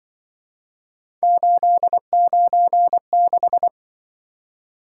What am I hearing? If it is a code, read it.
Morse code, heard as a keyed tone, about 24 wpm, 700 Hz. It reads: 896
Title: Speed-Racing - Three Letter-Number Combinations - 20wpm